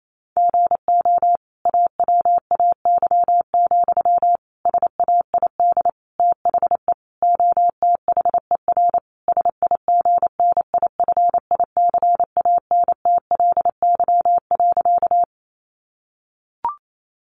Morse code, heard as a keyed tone, about 28 wpm, 700 Hz. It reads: GO AWAY, SAID THE OTHER SIGNIFICANTLY.
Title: Sentences from Top 1500 Words 28wpm